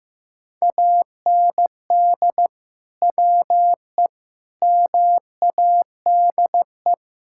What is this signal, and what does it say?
Morse code, heard as a keyed tone, about 15 wpm, 700 Hz. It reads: AND WE MADE
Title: Sentences from Top 1600 Words 15wpm